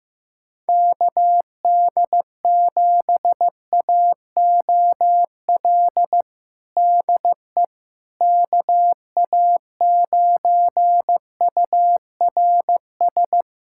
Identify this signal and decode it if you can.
Morse code, heard as a keyed tone, about 16 wpm, 700 Hz. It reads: KD7AOL DE KA9URS
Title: USA Callsign DE Callsign 15wpm